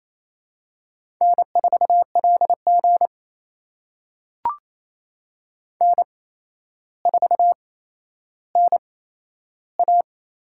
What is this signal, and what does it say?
Morse code, heard as a keyed tone, about 28 wpm, spaced as for 12 wpm, 700 Hz. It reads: N4LG N4NA
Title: USA Call Signs - 4 Characters - 8x Character Spacing - 28wpm